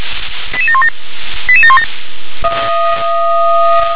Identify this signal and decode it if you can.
Five-tone selective call (ZVEI), heard as quick runs of five tones, tones of about 70 ms. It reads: 88717 88717